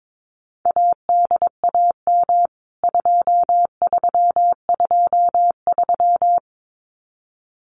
Morse code, heard as a keyed tone, about 22 wpm, 700 Hz. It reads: ADAM 2323
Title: CWT Exchange 22wpm